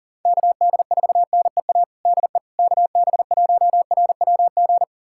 Morse code, heard as a keyed tone, about 40 wpm, 700 Hz. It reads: KD4NEA DE KB1RWG
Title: USA Callsign DE Callsign - 6 Characters - 40wpm